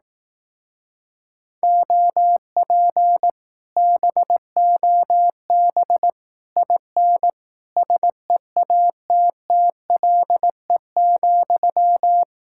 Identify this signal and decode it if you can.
Morse code, heard as a keyed tone, about 18 wpm, 700 Hz. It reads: OP BOB IN SEATTLE,
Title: Speed-Racing Common QSO Phrases Encore - 15wpm